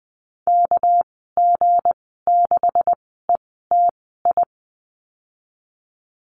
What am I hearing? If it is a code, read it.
Morse code, heard as a keyed tone, about 20 wpm, spaced as for 15 wpm, 700 Hz. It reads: KG6ETI
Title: 207 Lesson 33g - Letter j - Call Signs 20wpm